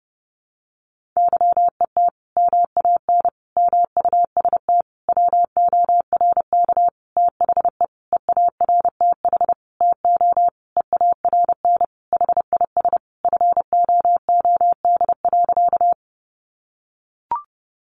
Morse code, heard as a keyed tone, about 30 wpm, 700 Hz. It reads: YET MAN MUST WORK THE EARTH TO EARN HIS FOOD.
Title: Sentences from Top 1500 Words 30wpm